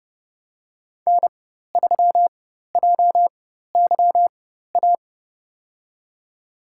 Morse code, heard as a keyed tone, about 31 wpm, spaced as for 14 wpm, 700 Hz. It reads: N3JYA
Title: USA Call Signs - 5 Characters - 4x Character Spacing - 30wpm